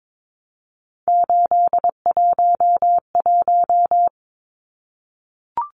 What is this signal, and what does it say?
Morse code, heard as a keyed tone, about 22 wpm, 700 Hz. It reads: 811
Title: Numbers - 3-Digit 22wpm